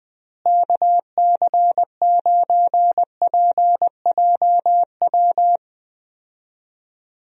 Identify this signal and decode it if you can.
Morse code, heard as a keyed tone, about 20 wpm, 700 Hz. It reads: KC9PJW
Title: USA Call Signs - 6 Characters - Repeated 3x - 20wpm